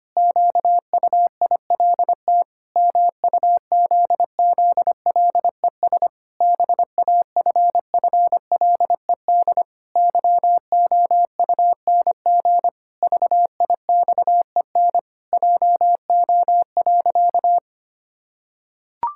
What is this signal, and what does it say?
Morse code, heard as a keyed tone, about 25 wpm, 700 Hz. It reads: QUILT MUZZLES BAFFLED YOUNG VIXEN JO.
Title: Pangram Challenge v2 - 25wpm